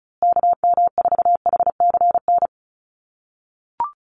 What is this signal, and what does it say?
Morse code, heard as a keyed tone, about 34 wpm, 700 Hz. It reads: KM4HCN
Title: USA Call Signs Repeated 2x 35wpm